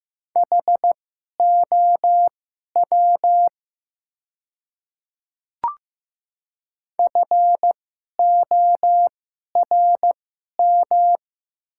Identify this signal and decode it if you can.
Morse code, heard as a keyed tone, about 15 wpm, spaced as for 11 wpm, 700 Hz. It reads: HOW FORM
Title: 2x Character Spacing - Top 200 Words - 15wpm